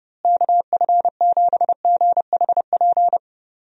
Morse code, heard as a keyed tone, about 30 wpm, 700 Hz. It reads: KF7GHP
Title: USA Call Signs - 6 Characters - Repeated 2x - 30wpm